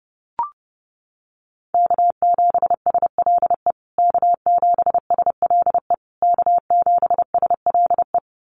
Morse code, heard as a keyed tone, about 30 wpm, 700 Hz. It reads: K7SLE K7SLE K7SLE
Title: USA Call Signs Repeated 3x 30wpm